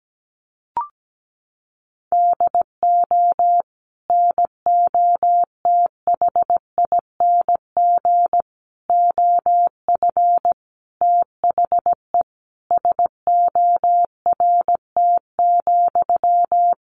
Morse code, heard as a keyed tone, about 17 wpm, 700 Hz. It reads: DO NOTHING OF THE SORT,
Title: Sentences from Top 1300 Words 17wpm